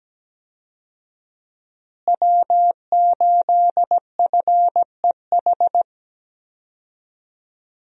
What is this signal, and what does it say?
Morse code, heard as a keyed tone, about 17 wpm, 700 Hz. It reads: W8FEH
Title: USA Call Signs - 5 Characters - Repeated 2x - 17wpm